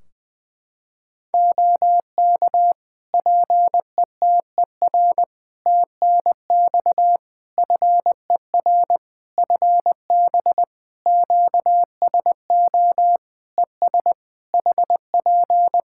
Morse code, heard as a keyed tone, about 20 wpm, 700 Hz. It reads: OK PETER TNX FER FB QSO ES HP
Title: K1USN SST QSO Practice 20wpm